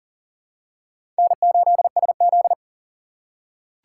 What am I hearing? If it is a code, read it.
Morse code, heard as a keyed tone, about 40 wpm, 700 Hz. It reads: N8SZ
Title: USA Call Signs - 4 Characters - 6x Character Spacing - 40wpm